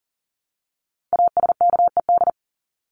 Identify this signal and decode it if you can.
Morse code, heard as a keyed tone, about 40 wpm, 700 Hz. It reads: ASKED
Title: ED Suffix - Top 100 Words with Base Words 40wpm